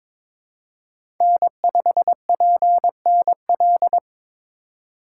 Morse code, heard as a keyed tone, about 22 wpm, 700 Hz. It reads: N5PNL
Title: USA Call Signs - 5 Characters - 4x Character Spacing - 22wpm